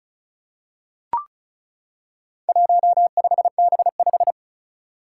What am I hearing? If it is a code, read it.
Morse code, heard as a keyed tone, about 36 wpm, 700 Hz. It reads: 15B5
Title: Four Challenging Letter-Number Combinations 35wpm